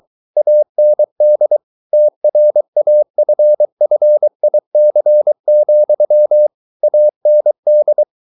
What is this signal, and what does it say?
Morse code, heard as a keyed tone, about 23 wpm, 600 Hz. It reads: AND TRAFFIC, AND